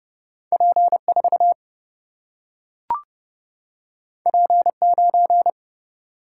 Morse code, heard as a keyed tone, about 30 wpm, 700 Hz. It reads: P4 P9
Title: Single-Letter Single-Number 30wpm